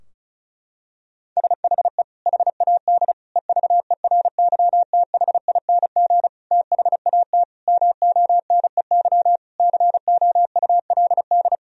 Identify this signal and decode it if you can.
Morse code, heard as a keyed tone, about 36 wpm, 700 Hz. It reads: SHE HAD EVERYTHING THAT MONEY COULD
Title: Sentences from Top 1000 Words - Review Part 4 - 35wpm